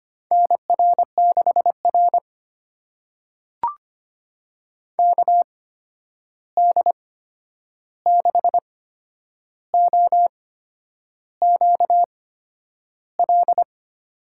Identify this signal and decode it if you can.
Morse code, heard as a keyed tone, about 25 wpm, spaced as for 9 wpm, 700 Hz. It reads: NR6R KD6OQL
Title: USA Call Signs - 8x Character Spacing - 25wpm